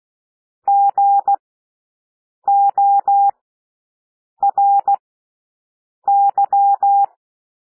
Morse code, heard as a keyed tone, about 16 wpm, spaced as for 6 wpm, 800 Hz. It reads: GORY
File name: cwmp3 / W1 SOUND WORDS 16 wpm